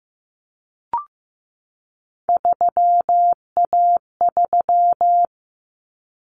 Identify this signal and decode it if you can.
Morse code, heard as a keyed tone, about 15 wpm, 700 Hz. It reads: 3A3
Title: Common Cut Numbers - 3-Digit 15wpm